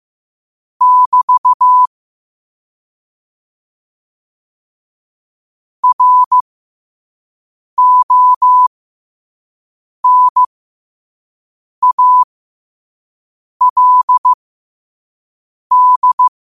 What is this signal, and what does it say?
Morse code, heard as a keyed tone, about 15 wpm, spaced as for 5 wpm, 1000 Hz. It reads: <BT> RONALD
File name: qotdfiles / 5wpm